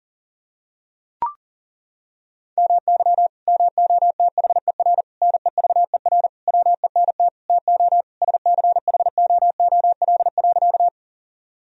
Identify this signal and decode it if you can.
Morse code, heard as a keyed tone, about 40 wpm, 700 Hz. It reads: MY MOTHER NEVER WENT TO SCHOOL.
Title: Sentences from Top 1000 Words - Review Part 1 - 40wpm